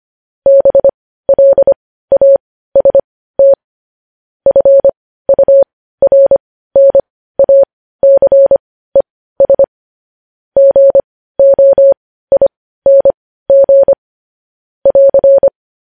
Morse code, heard as a keyed tone, about 25 wpm, spaced as for 15 wpm, 550 Hz. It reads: BLAST FURNACES GOING <AR>
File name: podcast / News headlines-1x-25x15